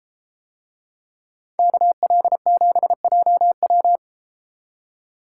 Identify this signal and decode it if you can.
Morse code, heard as a keyed tone, about 31 wpm, 700 Hz. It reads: KL7JW